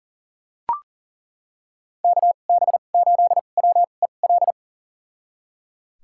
Morse code, heard as a keyed tone, about 40 wpm, spaced as for 29 wpm, 700 Hz. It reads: KB8WEL
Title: USA Call Signs - 6 Characters - 2x Character Spacing - 40wpm